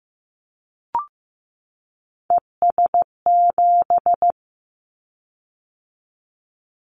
Morse code, heard as a keyed tone, about 15 wpm, 700 Hz. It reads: ES7
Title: Callsign Prefixes for Top 50 DXCC - 15wpm